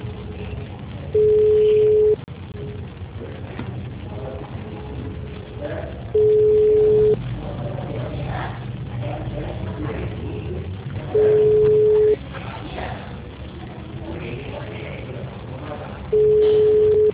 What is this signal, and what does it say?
Ringback tone (Europe, 425 Hz), with speech.